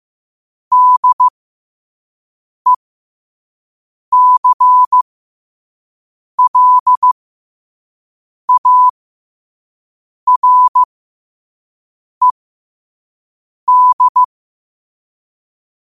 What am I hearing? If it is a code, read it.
Morse code, heard as a keyed tone, about 15 wpm, spaced as for 5 wpm, 1000 Hz. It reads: DECLARED
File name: qotdfiles / 5wpm